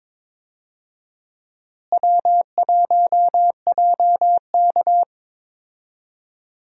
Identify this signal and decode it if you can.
Morse code, heard as a keyed tone, about 22 wpm, 700 Hz. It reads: W1JK
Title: USA Call Signs - 4 Characters - Repeated 3x - 22wpm